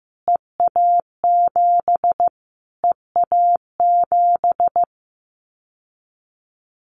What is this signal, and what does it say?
Morse code, heard as a keyed tone, about 15 wpm, 700 Hz. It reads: EA7 EA7
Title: Callsign Prefixes for Top 25 DXCC - Repeated 3x - 15wpm